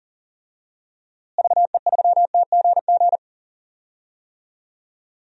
Morse code, heard as a keyed tone, about 40 wpm, 700 Hz. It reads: VE3TGG
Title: DXCC-100 Call Signs - 6 Characters - Repeated 3x - 40wpm